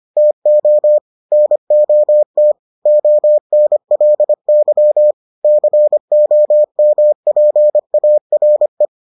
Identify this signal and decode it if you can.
Morse code, heard as a keyed tone, about 25 wpm, 600 Hz. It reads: TO NOT ONLY COMPARE